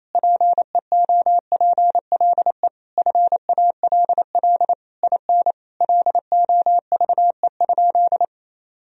Morse code, heard as a keyed tone, about 28 wpm, 700 Hz. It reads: PEOPLE FALL IN LOVE?
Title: Sentences from Top 1000 Words - Review Part 2 - 28wpm